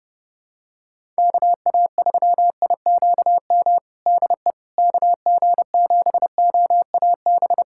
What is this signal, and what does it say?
Morse code, heard as a keyed tone, about 32 wpm, 700 Hz. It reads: KA3IQM DE KG7OAB
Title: USA Callsign DE Callsign - 6 Characters - 30wpm